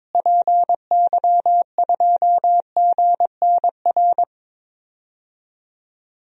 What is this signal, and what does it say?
Morse code, heard as a keyed tone, about 22 wpm, 700 Hz. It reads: PY2GNR
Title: DXCC-100 Call Signs - 6 Characters - Repeated 2x - 22wpm